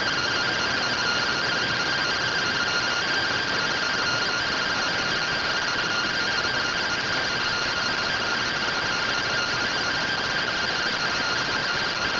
Two-tone teleprinter signal, about 170 Hz shift.